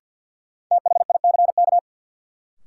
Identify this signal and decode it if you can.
Morse code, heard as a keyed tone, about 52 wpm, 700 Hz. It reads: THICK
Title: Sentences from Top 1600 Words - New Word Drill - 50wpm